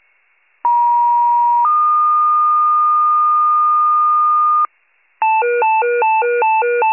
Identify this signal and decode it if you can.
Two-tone page, heard as a one-second tone, then a three-second tone, about 950 then 1220 Hz, with noise. It is heard from 0.6 to 4.7 s.
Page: County Wide Paging Archive